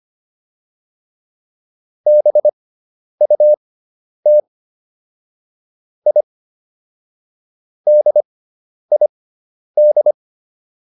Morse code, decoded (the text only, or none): BUT I DID